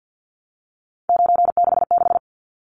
Morse code, heard as a keyed tone, about 50 wpm, 700 Hz. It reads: 966
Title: Numbers - 3-Digit 50wpm